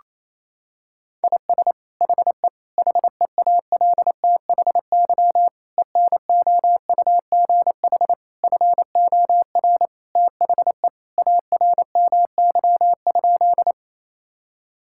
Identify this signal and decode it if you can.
Morse code, heard as a keyed tone, about 28 wpm, 700 Hz. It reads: IS HE HEALTHY ENOUGH FOR THE ARMY?
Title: Sentences from Top 1000 Words - Review Part 5 - 28wpm